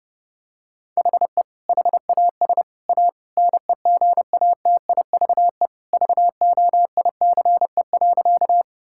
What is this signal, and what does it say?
Morse code, heard as a keyed tone, about 32 wpm, 700 Hz. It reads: HE HAS A NEGATIVE VOICE.
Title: Sentences from Top 1000 Words - Review Part 6 - 30wpm